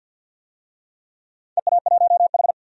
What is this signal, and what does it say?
Morse code, heard as a keyed tone, about 52 wpm, 700 Hz. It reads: EA1H
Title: DXCC-100 Call Signs - 4 Characters - 50wpm